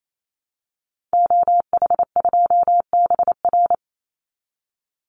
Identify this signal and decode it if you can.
Morse code, heard as a keyed tone, about 28 wpm, 700 Hz. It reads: OH2BR
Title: Call Signs Repeated 3x - 28wpm